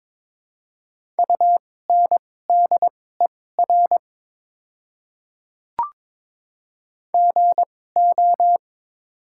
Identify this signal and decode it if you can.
Morse code, heard as a keyed tone, about 22 wpm, spaced as for 16 wpm, 690 Hz. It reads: UNDER GO
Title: 2x Character Spacing - Top 200 Words - 22wpm